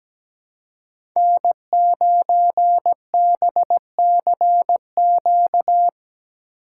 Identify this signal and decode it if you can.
Morse code, heard as a keyed tone, about 17 wpm, 700 Hz. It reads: N9BCQ
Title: USA Call Signs - 5 Characters - 4x Character Spacing - 17wpm